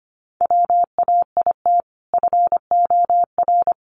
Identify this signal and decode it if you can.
Morse code, heard as a keyed tone, about 25 wpm, 700 Hz. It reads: WAIT FOR
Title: Sentences from Top 500 Words 25wpm